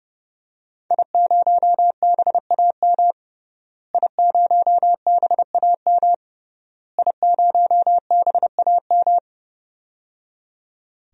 Morse code, heard as a keyed tone, about 29 wpm, 700 Hz. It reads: I0BAM I0BAM I0BAM
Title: DXCC-100 Call Signs - 5 Characters - Repeated 3x - 30wpm